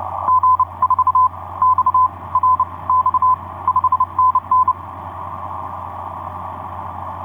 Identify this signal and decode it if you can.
Morse code, heard as a keyed tone, about 29 wpm, spaced as for 22 wpm, 1020 Hz. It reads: G4XRX 5NN
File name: z21bb g4xrx etc 10m cw 23oct2010 1201